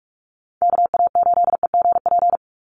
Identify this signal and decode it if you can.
Morse code, heard as a keyed tone, about 44 wpm, 700 Hz. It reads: KA8EGP
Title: USA Call Signs 45wpm